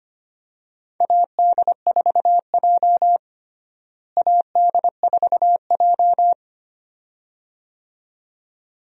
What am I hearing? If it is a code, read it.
Morse code, heard as a keyed tone, about 25 wpm, 700 Hz. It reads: AD4J AD4J
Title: USA Call Signs - 4 Characters - Repeated 2x - 25wpm